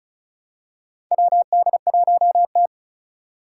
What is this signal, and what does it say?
Morse code, heard as a keyed tone, about 36 wpm, 690 Hz. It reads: WD1T
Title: Four Challenging Letter-Number Combinations 35wpm